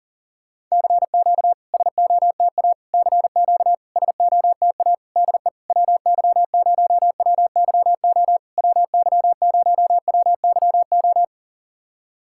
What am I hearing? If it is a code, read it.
Morse code, heard as a keyed tone, about 42 wpm, 690 Hz. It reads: CQ SOTA CQ SOTA DE WY0WYO WY0WYO
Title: SOTA QSO Practice 40wpm